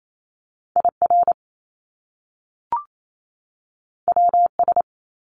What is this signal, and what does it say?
Morse code, heard as a keyed tone, about 28 wpm, 700 Hz. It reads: IR WS
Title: Top Two Letter Combinations in Words 28wpm